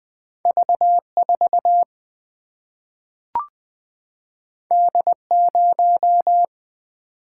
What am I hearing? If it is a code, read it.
Morse code, heard as a keyed tone, about 20 wpm, 700 Hz. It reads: V4 D0
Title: Single-Letter Single-Number 20wpm